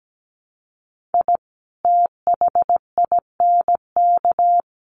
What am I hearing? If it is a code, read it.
Morse code, heard as a keyed tone, about 17 wpm, 700 Hz. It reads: I THINK
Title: Sets of 3 Words 17wpm